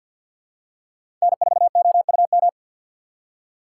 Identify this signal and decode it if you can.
Morse code, heard as a keyed tone, about 47 wpm, 700 Hz. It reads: N4OUM
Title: USA Call Signs - 5 Characters - 3x Character Spacing - 50wpm